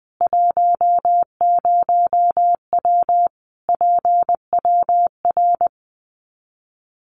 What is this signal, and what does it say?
Morse code, heard as a keyed tone, about 20 wpm, 700 Hz. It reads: 10W PWR
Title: Common QSO Phrases 20wpm